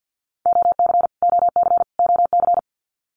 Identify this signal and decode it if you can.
Morse code, heard as a keyed tone, about 50 wpm, 700 Hz. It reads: OC OC OC
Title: Abbreviations Repeated 3x - 50wpm